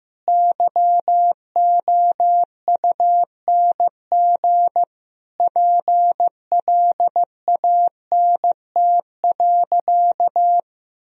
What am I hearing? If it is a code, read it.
Morse code, heard as a keyed tone, about 15 wpm, 700 Hz. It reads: YOUNG PLANT.
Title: Sentences from Top 1000 Words - Review Part 1 - 15wpm